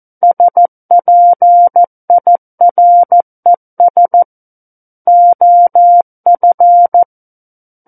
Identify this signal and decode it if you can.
Morse code, heard as a keyed tone, about 14 wpm, 700 Hz. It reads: SPIRES OF
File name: Book One / 05-Chapter 5 - 14 WPM